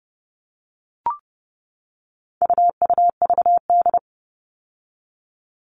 Morse code, heard as a keyed tone, about 30 wpm, 700 Hz. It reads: UUVD
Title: All Cut Numbers - 4-Digit 30wpm